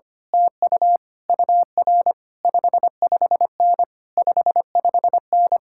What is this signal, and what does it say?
Morse code, heard as a keyed tone, about 24 wpm, 700 Hz. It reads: TU UR 55N 55N
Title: Speed-Racing Common QSO Phrases - 25wpm